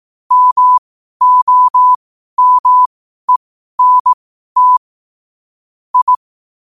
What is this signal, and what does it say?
Morse code, heard as a keyed tone, about 18 wpm, spaced as for 12 wpm, 1000 Hz. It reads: MOMENT I
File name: qotdfiles / 12wpm